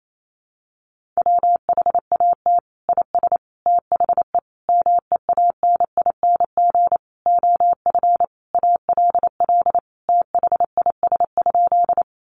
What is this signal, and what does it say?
Morse code, heard as a keyed tone, about 28 wpm, 690 Hz. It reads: WHAT IS THE MEANING OF ALL THIS?
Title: Sentences from Top 900 Words 28wpm